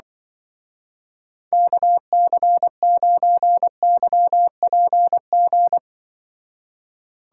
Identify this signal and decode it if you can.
Morse code, heard as a keyed tone, about 23 wpm, 700 Hz. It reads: KC9YPG
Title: Speed Racing - USA 2x3 Call Signs 20wpm